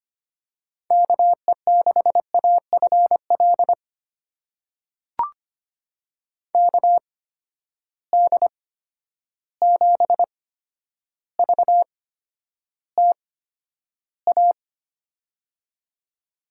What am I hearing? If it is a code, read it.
Morse code, heard as a keyed tone, about 25 wpm, spaced as for 11 wpm, 700 Hz. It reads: KE6AFL KD7VTA
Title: USA Call Signs - 6 Characters - 8x Character Spacing - 25wpm